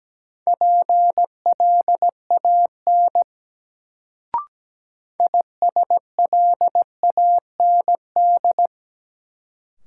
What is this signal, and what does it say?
Morse code, heard as a keyed tone, about 17 wpm, 700 Hz. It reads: PLAN ISLAND
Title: Sentences from Top 500 Words - New Word Drill - 17wpm